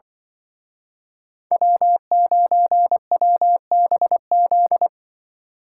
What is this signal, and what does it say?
Morse code, heard as a keyed tone, about 25 wpm, 700 Hz. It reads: W9WBZ
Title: Speed Racing - USA 1x3 Call Signs 20wpm